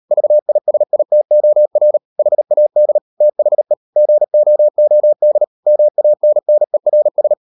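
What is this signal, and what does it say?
Morse code, heard as a keyed tone, about 38 wpm, 600 Hz. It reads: VISITOR HAD THE GOOD MANNERS
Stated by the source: Morse code practice at 38 WPM, sending Ken Follett's novel Never in English, from CW med Gnister.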